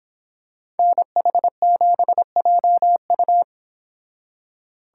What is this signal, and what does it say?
Morse code, heard as a keyed tone, about 25 wpm, 700 Hz. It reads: NH7JU